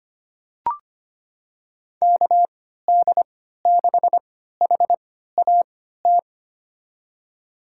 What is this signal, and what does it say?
Morse code, heard as a keyed tone, about 25 wpm, spaced as for 14 wpm, 700 Hz. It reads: KD6HAT
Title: USA Call Signs - 3x Character Spacing - 25wpm